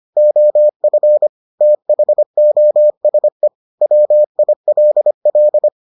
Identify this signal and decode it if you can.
Morse code, heard as a keyed tone, about 25 wpm, 600 Hz. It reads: OF THOSE WILL